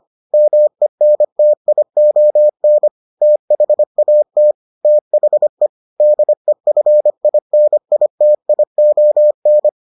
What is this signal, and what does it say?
Morse code, heard as a keyed tone, about 25 wpm, 600 Hz. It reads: MENTION THAT THE DEFINITION